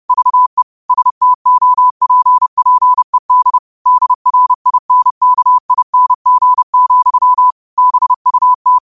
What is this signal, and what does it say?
Morse code, heard as a keyed tone, about 30 wpm, 1000 Hz. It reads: VE STOPPED DRINKING, BUT
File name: qotdfiles / 30wpm